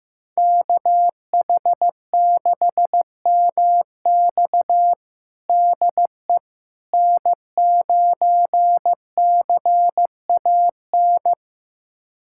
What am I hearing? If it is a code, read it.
Morse code, heard as a keyed tone, about 16 wpm, 700 Hz. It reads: KH6MX DE N9CAN
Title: USA Callsign DE Callsign - 5 Characters - 15wpm